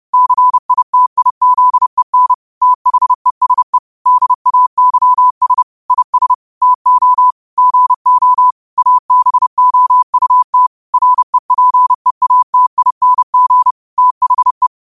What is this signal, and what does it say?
Morse code, heard as a keyed tone, about 30 wpm, 1000 Hz. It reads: CITIZEN THESE DAYS IS TO GO ABOUT REPEATING THE
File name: qotdfiles / 30wpm